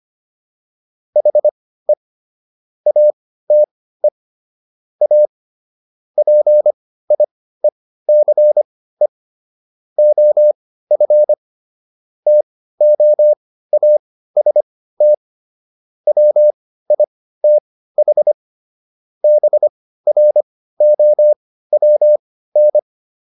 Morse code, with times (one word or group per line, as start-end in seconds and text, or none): 1.2-1.9 HE
2.9-4.1 ATE
5.0-5.3 A
6.2-9.1 PIECE
10.0-11.3 OF
12.3-15.1 TOAST
16.1-18.3 WITH
19.2-22.8 BROWN